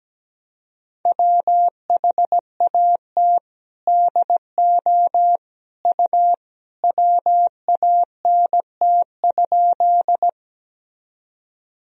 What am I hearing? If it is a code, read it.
Morse code, heard as a keyed tone, about 17 wpm, 700 Hz. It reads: WHAT DO U WANT?
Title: Sentences from Top 200 Words and Abbreviations 17wpm